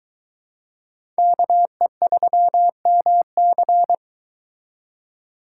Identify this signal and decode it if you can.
Morse code, heard as a keyed tone, about 23 wpm, 700 Hz. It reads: KE3MC